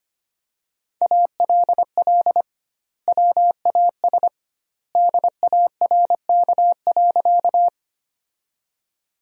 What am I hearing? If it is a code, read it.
Morse code, heard as a keyed tone, about 25 wpm, 700 Hz. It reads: ALL WAS DARK.
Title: 2x Word-Spacing - Sentences from Top 500 Words - 25wpm